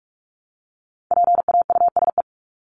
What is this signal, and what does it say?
Morse code, heard as a keyed tone, about 44 wpm, 700 Hz. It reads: PAUSE
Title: Top 1501-1750 Words 45wpm